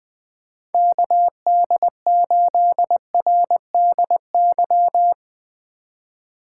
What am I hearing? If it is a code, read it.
Morse code, heard as a keyed tone, about 20 wpm, 700 Hz. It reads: KD8RDY
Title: USA Call Signs - 6 Characters - 2x Character Spacing - 20wpm